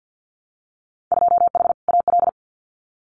Morse code, heard as a keyed tone, about 50 wpm, 700 Hz. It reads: 2H AL
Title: Winter Field Day Exchanges 50wpm